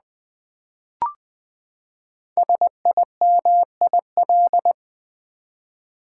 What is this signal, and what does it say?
Morse code, heard as a keyed tone, about 20 wpm, 700 Hz. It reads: SIMIL
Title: Build A Word - 5 Letters At A Time - 10-Letter Words - 20wpm